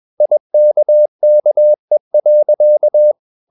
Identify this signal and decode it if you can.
Morse code, heard as a keyed tone, about 21 wpm, 600 Hz. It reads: IKKE.
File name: Ferie QRS 21wpm